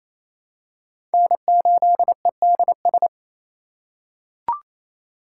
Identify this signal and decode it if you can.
Morse code, heard as a keyed tone, about 28 wpm, 700 Hz. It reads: N8EDS
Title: USA Call Signs - 5 Characters - 2x Character Spacing - 28wpm